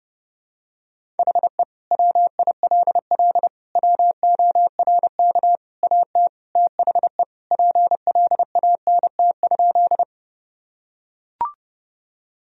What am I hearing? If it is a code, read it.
Morse code, heard as a keyed tone, about 30 wpm, 700 Hz. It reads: HE WILL WORK AT THE PLANT?
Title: Sentences from Top 1000 Words - Review Part 1 - 30wpm